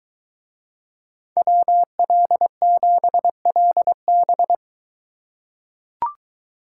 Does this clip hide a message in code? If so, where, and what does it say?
Yes: Morse code from 1.4 to 4.5 s, heard as a keyed tone, about 22 wpm, 700 Hz. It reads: WL7LB